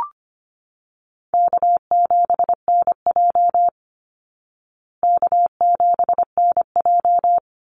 Morse code, heard as a keyed tone, about 25 wpm, 700 Hz. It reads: K7NJ K7NJ
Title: Easy Call Signs Repeated 2x - 25wpm